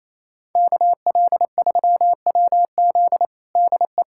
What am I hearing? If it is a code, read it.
Morse code, heard as a keyed tone, about 28 wpm, 700 Hz. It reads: KL3WZ DE
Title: USA Callsign DE Callsign - 5 Characters - 28wpm